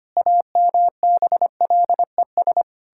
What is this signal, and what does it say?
Morse code, heard as a keyed tone, about 25 wpm, 700 Hz. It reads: AMBLES
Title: Pangram Challenge v2 - 25wpm